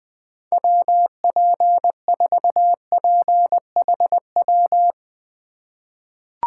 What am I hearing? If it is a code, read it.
Morse code, heard as a keyed tone, about 21 wpm, 700 Hz. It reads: WP4PHW
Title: USA Call Signs - 6 Characters - 3x Character Spacing - 20wpm